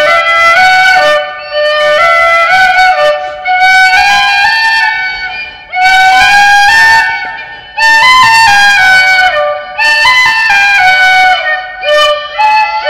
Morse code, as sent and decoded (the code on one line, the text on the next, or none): none